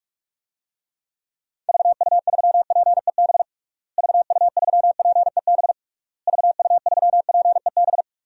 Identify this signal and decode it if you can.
Morse code, heard as a keyed tone, about 46 wpm, 700 Hz. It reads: VU3PEB VU3PEB VU3PEB
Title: DXCC-100 Call Signs - 6 Characters - Repeated 3x - 45wpm